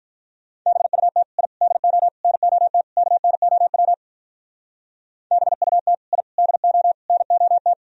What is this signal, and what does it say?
Morse code, heard as a keyed tone, about 42 wpm, 700 Hz. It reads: BUT I DO NOT KNOW BUT I DO NOT
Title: Speed-Racing - Sets of 5 Words 35wpm